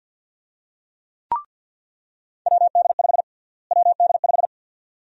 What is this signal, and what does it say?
Morse code, heard as a keyed tone, about 47 wpm, 700 Hz. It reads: WD5 WD5
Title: USA Call Sign Prefixes - Repeated 2x - 50wpm